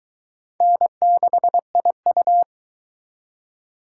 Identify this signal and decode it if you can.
Morse code, heard as a keyed tone, about 23 wpm, 700 Hz. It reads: N6IU